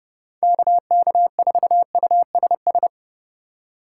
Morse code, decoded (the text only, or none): KK4USS